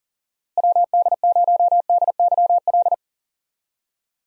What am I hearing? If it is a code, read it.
Morse code, heard as a keyed tone, about 38 wpm, 700 Hz. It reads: WD0DYL
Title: USA Call Signs - 6 Characters - Repeated 2x - 40wpm